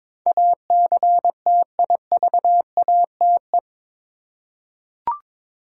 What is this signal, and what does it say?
Morse code, heard as a keyed tone, about 22 wpm, 700 Hz. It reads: ACTIVATE
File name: practice / 22wpm / Eight Letter Words - 22wpm